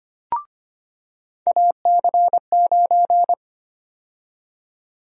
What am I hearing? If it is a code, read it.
Morse code, heard as a keyed tone, about 25 wpm, 700 Hz. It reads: AC9
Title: USA Call Sign Prefixes - 25wpm